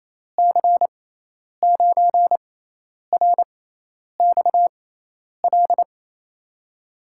Morse code, heard as a keyed tone, about 28 wpm, spaced as for 10 wpm, 700 Hz. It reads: C9RXL
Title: USA Call Signs - 6x Character Spacing - 28wpm